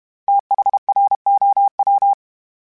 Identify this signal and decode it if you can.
Morse code, heard as a keyed tone, about 32 wpm, 800 Hz. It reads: THROW
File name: THs / TH / TH Prefixes 1 @30